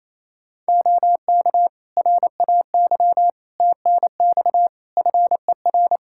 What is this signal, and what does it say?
Morse code, heard as a keyed tone, about 28 wpm, 700 Hz. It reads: OK RAY TNX FER
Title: Practice QSO - Exchange 4 - Ending - 28wpm